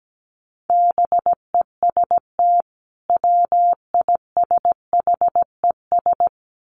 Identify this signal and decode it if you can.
Morse code, heard as a keyed tone, about 17 wpm, 700 Hz. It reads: BEST WISHES
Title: Common QSO Phrases 17wpm